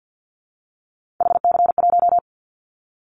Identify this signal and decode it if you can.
Morse code, heard as a keyed tone, about 50 wpm, 700 Hz. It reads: HC1
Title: Callsign Prefixes for Top 100 DXCC - 50wpm